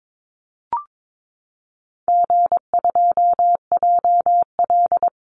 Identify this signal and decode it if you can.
Morse code, heard as a keyed tone, about 22 wpm, 700 Hz. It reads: G2JL
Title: Easy Call Signs - 22wpm